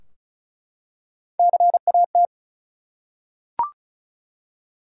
Morse code, heard as a keyed tone, about 36 wpm, 700 Hz. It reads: CAT
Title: Three Letter Words - 35wpm